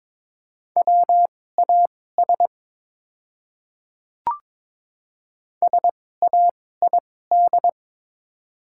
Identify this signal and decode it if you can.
Morse code, heard as a keyed tone, about 22 wpm, spaced as for 16 wpm, 700 Hz. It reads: WAS SAID
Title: 2x Character Spacing - Top 100 Words - 22wpm